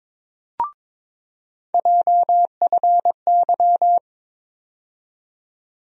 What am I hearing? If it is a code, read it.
Morse code, heard as a keyed tone, about 22 wpm, 700 Hz. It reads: JFY
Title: USA Call Sign Postfixes - 3 Letters 22wpm